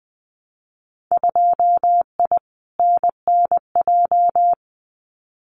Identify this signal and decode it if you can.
Morse code, heard as a keyed tone, about 20 wpm, 700 Hz. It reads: 2I NNJ
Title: Winter Field Day Exchanges Encore 20wpm